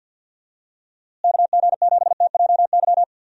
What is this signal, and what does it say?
Morse code, heard as a keyed tone, about 51 wpm, 700 Hz. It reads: KG7TJY
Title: USA Call Signs - 6 Characters - Repeated 3x - 50wpm